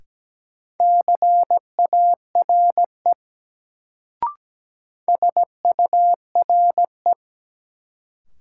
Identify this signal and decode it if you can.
Morse code, heard as a keyed tone, about 17 wpm, 700 Hz. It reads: CARE SURE
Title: Sets of 3 Words - New Word Drill - 17wpm